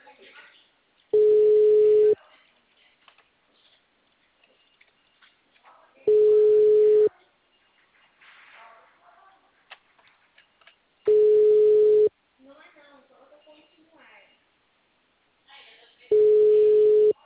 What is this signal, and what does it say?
Ringback tone (Europe, 425 Hz), with speech.
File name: out- -2053- - - .66176